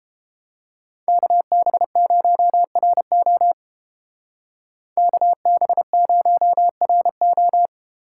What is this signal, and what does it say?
Morse code, heard as a keyed tone, about 32 wpm, 700 Hz. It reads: KB0RO KB0RO